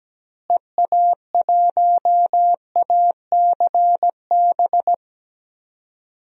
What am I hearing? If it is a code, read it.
Morse code, heard as a keyed tone, about 17 wpm, 700 Hz. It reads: EA1ACB
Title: DXCC-100 Call Signs - 6 Characters - Repeated 3x - 17wpm